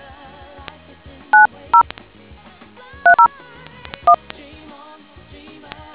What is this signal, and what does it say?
DTMF, heard as telephone touch tones, tones of about 100 ms. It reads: 90 301